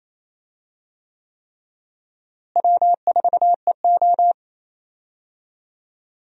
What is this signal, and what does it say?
Morse code, heard as a keyed tone, about 28 wpm, 700 Hz. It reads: W4EO
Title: USA Call Signs - 4 Characters - Repeated 3x - 28wpm